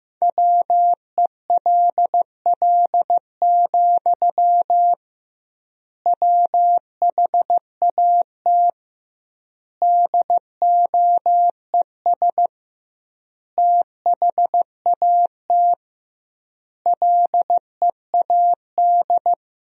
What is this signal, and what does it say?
Morse code, heard as a keyed tone, about 15 wpm, 700 Hz. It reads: WELL, WHAT DOES THAT LEAD
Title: 2x Word-Spacing - Sentences from Top 500 Words - 15wpm